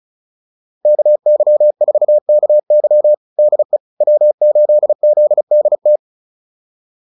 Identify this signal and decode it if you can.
Morse code, heard as a keyed tone, about 34 wpm, 600 Hz. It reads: KY4KY DE W8ZDT
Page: advanced-practice-files